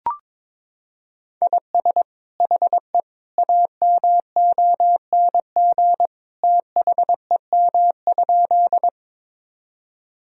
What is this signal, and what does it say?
Morse code, heard as a keyed tone, about 21 wpm, 700 Hz. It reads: IS HE AMONG THEM?
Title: Sentences from Top 1000 Words - Review Part 3 - 22wpm